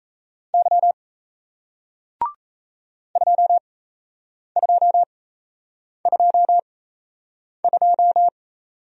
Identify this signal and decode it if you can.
Morse code, heard as a keyed tone, about 37 wpm, spaced as for 8 wpm, 700 Hz. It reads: Y 2222